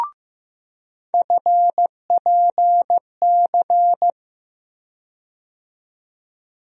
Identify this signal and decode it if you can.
Morse code, heard as a keyed tone, about 15 wpm, 700 Hz. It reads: FPC
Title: USA Call Sign Postfixes - 3 Letters 15wpm